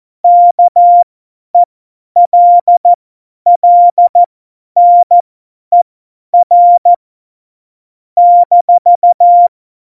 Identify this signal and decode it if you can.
Morse code, heard as a keyed tone, about 14 wpm, spaced as for 10 wpm, 700 Hz. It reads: KELLNER -
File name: Morsekurs / Witze / Witz 10. September 2019, 50 F70